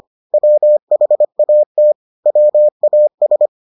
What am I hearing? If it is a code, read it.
Morse code, heard as a keyed tone, about 25 wpm, 600 Hz. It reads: WHAT WAS